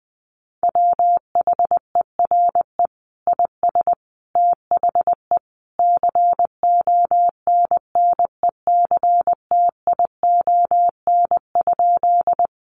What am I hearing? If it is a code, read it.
Morse code, heard as a keyed tone, about 20 wpm, 700 Hz. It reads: WHERE IS THE CONNECTION?
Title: Sentences from Top 1300 Words 20wpm